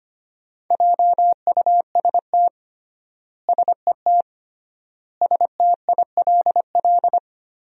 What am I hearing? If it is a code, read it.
Morse code, heard as a keyed tone, about 25 wpm, 700 Hz. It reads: JUST SET STILL
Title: 3x Word-Spacing - Sentences from Top 300 Words - 25wpm